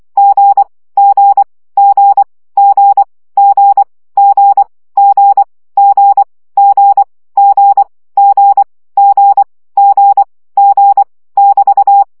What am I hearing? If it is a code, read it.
Morse code, heard as a keyed tone, about 24 wpm, 800 Hz. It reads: G G G G G G G G G G G G G G <BT>
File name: CW gyakorlas g 20 baud